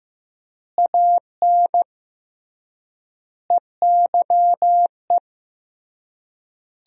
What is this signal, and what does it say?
Morse code, heard as a keyed tone, about 15 wpm, 700 Hz. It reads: AN EYE
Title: 3x Word-Spacing - Sentences from Top 300 Words - 15wpm